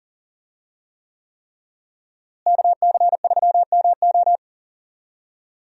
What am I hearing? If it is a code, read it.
Morse code, heard as a keyed tone, about 40 wpm, 700 Hz. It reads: KC3MO
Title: USA Call Signs - 5 Characters - Repeated 3x - 40wpm